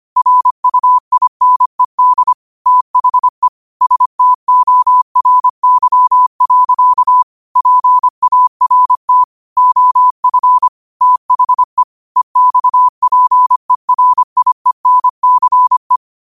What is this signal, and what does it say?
Morse code, heard as a keyed tone, about 25 wpm, 1000 Hz. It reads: RUINED THE STORY. PART OF THE EXPERIENCE
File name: qotdfiles / 25wpm